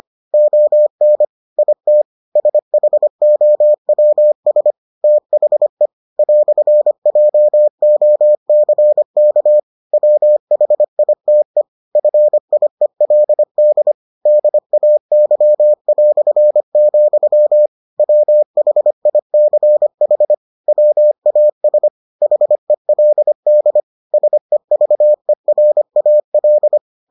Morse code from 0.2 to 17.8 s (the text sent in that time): ON IT SHOWS THE "JOCK WHITE FIELD DAY",